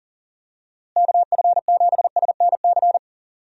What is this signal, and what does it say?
Morse code, heard as a keyed tone, about 40 wpm, 700 Hz. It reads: KF7SNC
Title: USA Call Signs - 6 Characters - Repeated 2x - 40wpm